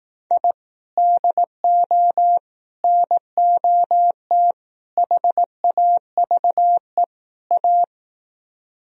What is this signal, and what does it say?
Morse code, heard as a keyed tone, about 18 wpm, 700 Hz. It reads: I DO NOT HAVE A
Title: Speed-Racing - Sets of 5 Words 15wpm